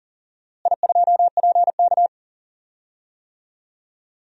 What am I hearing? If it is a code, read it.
Morse code, heard as a keyed tone, about 40 wpm, 700 Hz. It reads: I2PK
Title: DXCC-100 Call Signs - 4 Characters - Repeated 3x - 40wpm